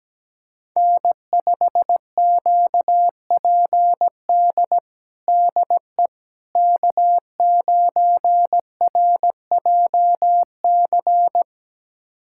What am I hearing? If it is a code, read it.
Morse code, heard as a keyed tone, about 17 wpm, 700 Hz. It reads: N5QPD DE K9RJC
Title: USA Callsign DE Callsign - 5 Characters - 17wpm